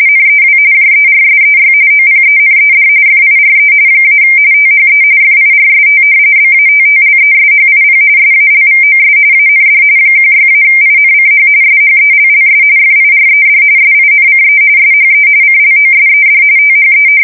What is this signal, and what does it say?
RTTY, about 45.45 baud. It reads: NEW JERSEY OPENED FIRE - THE FIRST BATTLESHIP SALVOS AND 16-INCH SHELLS FIRED SINCE SHE CLOSED THE
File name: fsk170 / battleshipnj